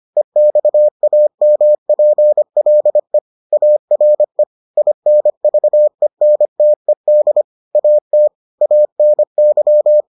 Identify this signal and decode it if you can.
Morse code, heard as a keyed tone, about 25 wpm, 600 Hz. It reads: EXAMPLE ARE INVENTED AT ANY